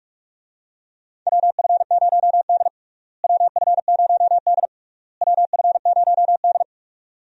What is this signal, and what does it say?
Morse code, heard as a keyed tone, about 46 wpm, 700 Hz. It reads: WF0D WF0D WF0D
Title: USA Call Signs - 4 Characters - Repeated 3x - 45wpm